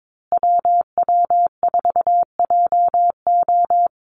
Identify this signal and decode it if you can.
Morse code, heard as a keyed tone, about 22 wpm, 700 Hz. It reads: WW4JO
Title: USA Call Signs Repeated 2x 22wpm